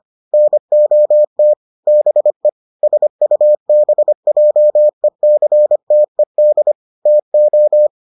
Morse code, heard as a keyed tone, about 25 wpm, 600 Hz. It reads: NOT BE SUBJECTED TO